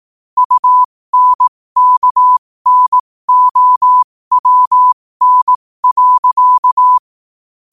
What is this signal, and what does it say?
Morse code, heard as a keyed tone, about 18 wpm, spaced as for 15 wpm, 1000 Hz. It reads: UNKNOWN.
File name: qotdfiles / 15wpm